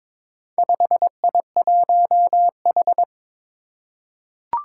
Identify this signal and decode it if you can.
Morse code, heard as a keyed tone, about 22 wpm, 700 Hz. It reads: 5I1H
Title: Four Challenging Letter-Number Combinations 22wpm